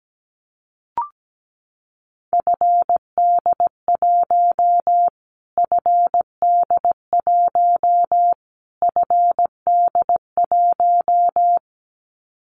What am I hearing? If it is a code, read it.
Morse code, heard as a keyed tone, about 17 wpm, 700 Hz. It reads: FD1 FD1 FD1
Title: Callsign Prefixes for Top 25 DXCC - Repeated 3x - 17wpm